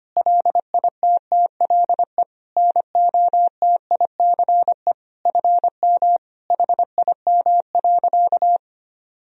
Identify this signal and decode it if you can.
Morse code, heard as a keyed tone, about 25 wpm, 700 Hz. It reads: LITTLE NOTICE FM HIM.
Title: Sentences from Top 400 Words and Abbreviations 25wpm